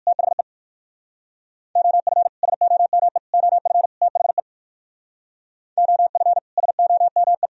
Morse code, heard as a keyed tone, about 66 wpm, 700 Hz. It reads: THE OF SOME OF THE OF SOME